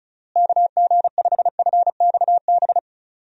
Morse code, heard as a keyed tone, about 36 wpm, 700 Hz. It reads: KG5FXB
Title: USA Call Signs - 6 Characters - Repeated 3x - 35wpm